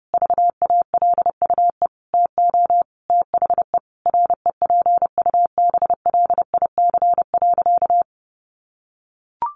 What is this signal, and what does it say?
Morse code, heard as a keyed tone, about 30 wpm, 700 Hz. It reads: VALUE TO THE REPUBLIC.
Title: Sentences from Top 1500 Words 30wpm